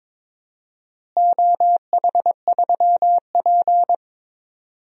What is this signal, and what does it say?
Morse code, heard as a keyed tone, about 22 wpm, 700 Hz. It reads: OH3P
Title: DXCC-100 Call Signs - 4 Characters - Repeated 2x - 22wpm